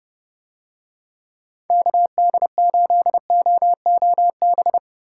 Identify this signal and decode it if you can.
Morse code, heard as a keyed tone, about 31 wpm, 700 Hz. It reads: KD8OOB